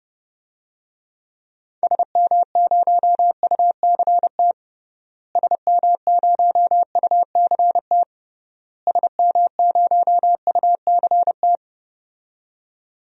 Morse code, heard as a keyed tone, about 31 wpm, 700 Hz. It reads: SM0UCT SM0UCT SM0UCT
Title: DXCC-100 Call Signs - 6 Characters - Repeated 3x - 30wpm